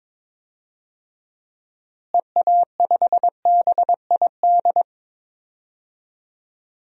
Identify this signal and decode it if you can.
Morse code, heard as a keyed tone, about 22 wpm, 700 Hz. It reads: EA5BID
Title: DXCC-100 Call Signs - 6 Characters - Repeated 2x - 22wpm